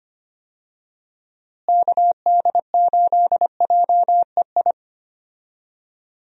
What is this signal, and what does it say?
Morse code, heard as a keyed tone, about 25 wpm, 700 Hz. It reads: KD8JEI
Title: USA Call Signs - 6 Characters - Repeated 3x - 25wpm